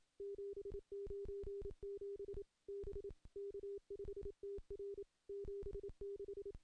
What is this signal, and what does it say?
Morse code, heard as a keyed tone, about 28 wpm, 400 Hz. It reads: 797 B K5TR 76
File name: CW-SS-k5tr